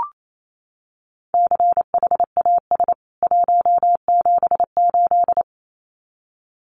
Morse code, heard as a keyed tone, about 28 wpm, 700 Hz. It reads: CHAS 178
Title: CWT Exchange 28wpm